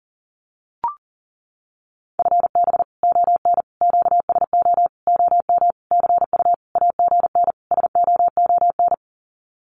Morse code, heard as a keyed tone, about 40 wpm, 700 Hz. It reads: FB ON QSO OM CU AGN SOON
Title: Common QSO Phrases 40wpm